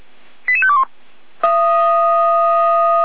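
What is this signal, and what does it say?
Five-tone selective call (ZVEI), heard as quick runs of five tones, tones of about 70 ms. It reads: 88521
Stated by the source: ZVEI five-tone selective call received on an SDR.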